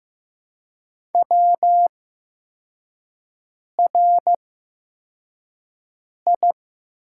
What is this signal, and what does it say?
Morse code, heard as a keyed tone, about 15 wpm, spaced as for 4 wpm, 700 Hz. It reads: WRI
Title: 8x Character Spacing - Top 200 Words - 15wpm